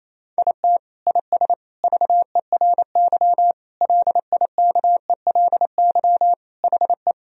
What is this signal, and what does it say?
Morse code, heard as a keyed tone, about 28 wpm, 700 Hz. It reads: IT IS VERY LIKELY HE
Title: Sentences from Top 1000 Words - Review Part 4 - 28wpm